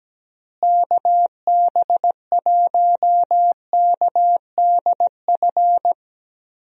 Morse code, heard as a keyed tone, about 17 wpm, 700 Hz. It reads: KB1KDF
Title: USA Call Signs - 6 Characters - Repeated 3x - 17wpm